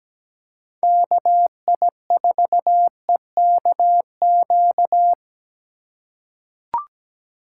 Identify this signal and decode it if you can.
Morse code, heard as a keyed tone, about 17 wpm, 700 Hz. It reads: KI4EKQ
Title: USA Call Signs - 4x Character Spacing - 17wpm